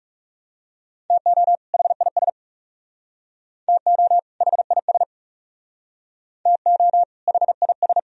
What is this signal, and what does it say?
Morse code, heard as a keyed tone, about 37 wpm, 700 Hz. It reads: TO HIS TO HIS TO HIS
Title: Speed-Racing - Sets of 2 Words 30wpm